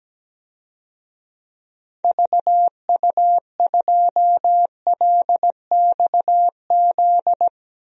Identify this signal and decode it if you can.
Morse code, heard as a keyed tone, about 17 wpm, 700 Hz. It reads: VU2LXZ
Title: DXCC-100 Call Signs - 6 Characters - 17wpm